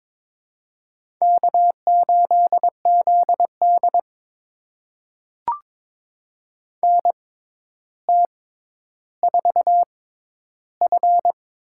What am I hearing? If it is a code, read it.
Morse code, heard as a keyed tone, about 22 wpm, spaced as for 11 wpm, 700 Hz. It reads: K8ZD NT4F
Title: USA Call Signs - 4 Characters - 6x Character Spacing - 22wpm